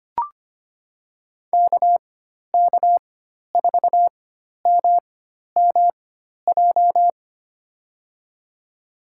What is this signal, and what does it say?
Morse code, heard as a keyed tone, about 25 wpm, spaced as for 12 wpm, 690 Hz. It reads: KK4MMJ
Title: USA Call Signs - 4x Character Spacing - 25wpm